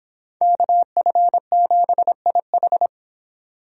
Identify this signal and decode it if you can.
Morse code, heard as a keyed tone, about 24 wpm, 700 Hz. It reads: KF7IH